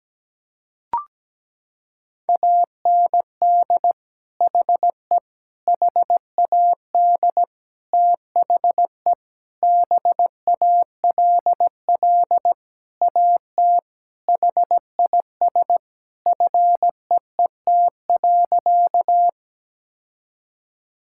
Morse code, heard as a keyed tone, about 17 wpm, 700 Hz. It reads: AND HE HAD THE BALL AT HIS FEET.
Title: Sentences from Top 1000 Words - Review Part 6 - 17wpm